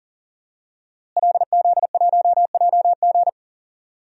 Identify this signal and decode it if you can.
Morse code, heard as a keyed tone, about 37 wpm, 700 Hz. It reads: LZ1JG
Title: DXCC-100 Call Signs - 5 Characters - Repeated 2x - 40wpm